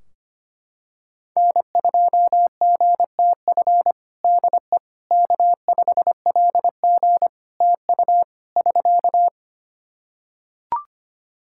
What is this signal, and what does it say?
Morse code, heard as a keyed tone, about 25 wpm, 700 Hz. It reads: N2GTF DE K5LG TU <SK>
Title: Practice QSO - Exchange 4 - Ending - 25wpm